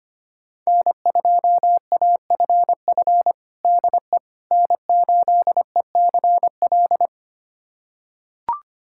Morse code, heard as a keyed tone, about 25 wpm, 700 Hz. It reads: N2AFF DE N8ECL
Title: USA Callsign DE Callsign - 5 Characters - 25wpm